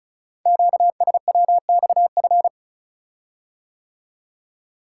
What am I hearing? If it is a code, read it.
Morse code, heard as a keyed tone, about 36 wpm, 700 Hz. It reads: QSWXF
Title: Code Groups - 5 Letters - 35wpm